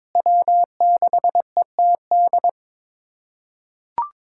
Morse code, heard as a keyed tone, about 22 wpm, 700 Hz. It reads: W6ETD
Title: USA Callsign DE Callsign - 5 Characters - 22wpm